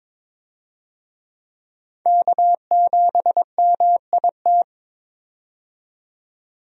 Morse code, heard as a keyed tone, about 22 wpm, 700 Hz. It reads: K7MIT
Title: USA Call Signs - 5 Characters - 22wpm